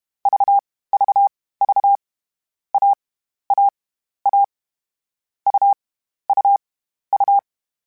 Morse code, heard as a keyed tone, about 31 wpm, spaced as for 14 wpm, 800 Hz. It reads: VVV AAA UUU
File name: Difficult Combos / Difficult Combos-mp3s / AUV ShortRun @30